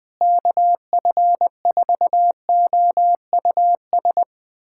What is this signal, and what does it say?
Morse code, heard as a keyed tone, about 20 wpm, 700 Hz. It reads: KF4OUS
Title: USA Callsign DE Callsign - 6 Characters - 20wpm